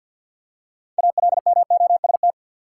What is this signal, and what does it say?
Morse code, heard as a keyed tone, about 47 wpm, 700 Hz. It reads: ALMOST
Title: Six Letter Words - 50wpm